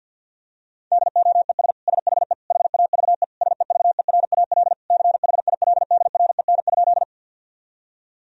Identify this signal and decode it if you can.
Morse code, heard as a keyed tone, about 47 wpm, 700 Hz. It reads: DOES SHE HAVE SEVERAL CHILDREN?
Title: Sentences from Top 1000 Words - Review Part 2 - 50wpm